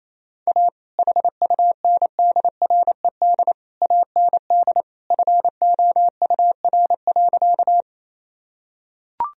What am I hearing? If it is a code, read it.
Morse code, heard as a keyed tone, about 28 wpm, 700 Hz. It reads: A HUNDRED AND FOUR.
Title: Sentences from Top 1000 Words - Review Part 2 - 28wpm